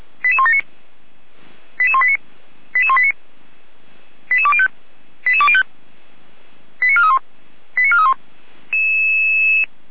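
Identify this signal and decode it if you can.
Five-tone selective call (ZVEI), heard as quick runs of five tones, tones of about 70 ms. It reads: 88179 88189 88189 88205 88205 89431 89431
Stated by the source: ZVEI five-tone call 88169 received by SDR, from audio server of Feuerwehr Orlamünde (fire brigade).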